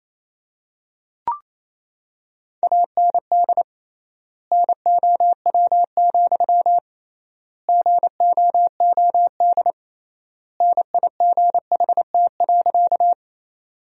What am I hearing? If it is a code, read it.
Morse code, heard as a keyed tone, about 28 wpm, 700 Hz. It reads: AND NOW, GOOD NIGHT.
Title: 3x Word-Spacing - Sentences from Top 300 Words - 28wpm